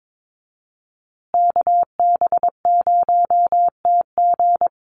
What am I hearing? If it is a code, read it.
Morse code, heard as a keyed tone, about 22 wpm, 700 Hz. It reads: KB0TG
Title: USA Call Signs Repeated 3x 22wpm